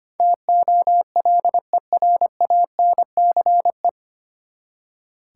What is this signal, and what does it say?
Morse code, heard as a keyed tone, about 25 wpm, 700 Hz. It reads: TOLERANCE
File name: Nine Letter Words - 25wpm